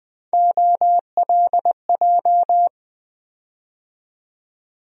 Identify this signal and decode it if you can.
Morse code, heard as a keyed tone, about 20 wpm, 700 Hz. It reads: OLJ
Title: USA Call Sign Postfixes - 3 Letters 20wpm